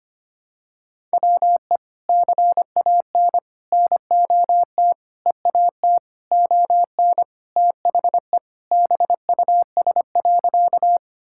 Morse code, heard as a keyed tone, about 25 wpm, 700 Hz. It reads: WE CAN NOT EAT ON THE BUS.
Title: Sentences from Top 1000 Words - Review Part 6 - 25wpm